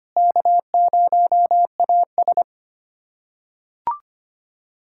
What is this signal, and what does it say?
Morse code, heard as a keyed tone, about 25 wpm, 700 Hz. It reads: K0AS
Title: USA Callsign DE Callsign - 4 Characters - 25wpm